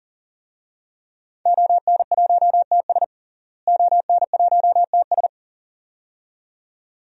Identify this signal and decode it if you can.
Morse code, heard as a keyed tone, about 40 wpm, 700 Hz. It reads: ON1TS ON1TS
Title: DXCC-100 Call Signs - 5 Characters - Repeated 2x - 40wpm